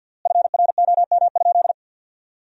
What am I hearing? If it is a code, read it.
Morse code, heard as a keyed tone, about 52 wpm, 700 Hz. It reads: FROM?
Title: 3x Word-Spacing - Sentences from Top 200 Words - 50wpm